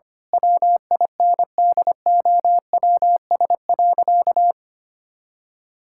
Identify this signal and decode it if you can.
Morse code, heard as a keyed tone, about 25 wpm, 700 Hz. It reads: WINDOWS.
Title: Pangram Challenge v2 - 25wpm